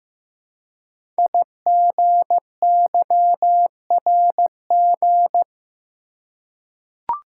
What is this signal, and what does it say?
Morse code, heard as a keyed tone, about 15 wpm, 700 Hz. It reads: IGYRG
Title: Code Groups - 5 Letters - 15wpm